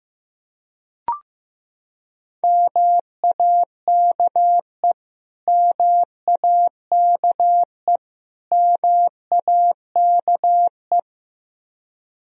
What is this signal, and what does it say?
Morse code, heard as a keyed tone, about 15 wpm, 700 Hz. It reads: MAKE MAKE MAKE
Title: Top 100 Words Repeated 15wpm